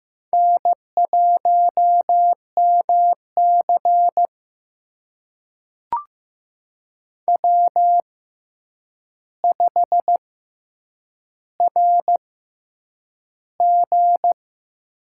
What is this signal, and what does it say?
Morse code, heard as a keyed tone, about 15 wpm, spaced as for 8 wpm, 700 Hz. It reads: N1MC W5RG
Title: USA Call Signs - 4 Characters - 6x Character Spacing - 15wpm